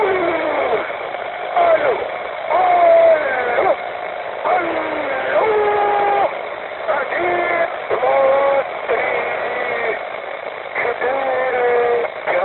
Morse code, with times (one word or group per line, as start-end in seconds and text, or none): none